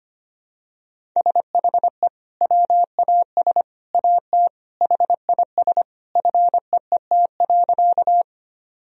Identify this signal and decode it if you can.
Morse code, heard as a keyed tone, about 25 wpm, 700 Hz. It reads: SHE WAS AT HIS FEET.
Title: Sentences from Top 1000 Words - Review Part 1 - 25wpm